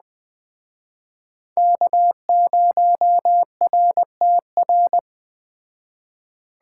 Morse code, heard as a keyed tone, about 20 wpm, 700 Hz. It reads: K0RTR
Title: Speed Racing - USA 1x3 Call Signs 17wpm